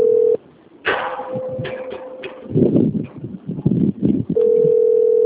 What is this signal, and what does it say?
Ringback tone (North America), 440 + 480 Hz.
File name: rg-1600-unknown- - - .39929